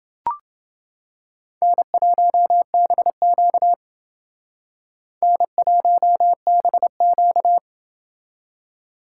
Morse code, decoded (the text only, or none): N1BQ N1BQ